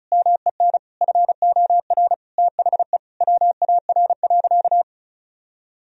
Morse code, heard as a keyed tone, about 36 wpm, 700 Hz. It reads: MEN FOR THE WAR.
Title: Sentences from Top 1000 Words - Review Part 4 - 35wpm